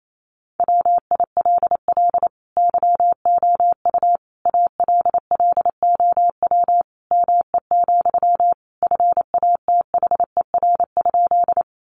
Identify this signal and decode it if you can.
Morse code, heard as a keyed tone, about 28 wpm, 700 Hz. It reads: WILL YOU ALLOW ME, FATHER?
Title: Sentences from Top 700 Words 28wpm